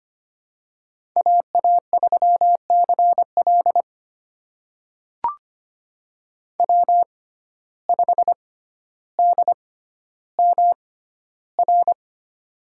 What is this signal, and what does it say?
Morse code, heard as a keyed tone, about 25 wpm, spaced as for 13 wpm, 700 Hz. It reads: AA3CL W5DMR
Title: USA Call Signs - 5 Characters - 6x Character Spacing - 25wpm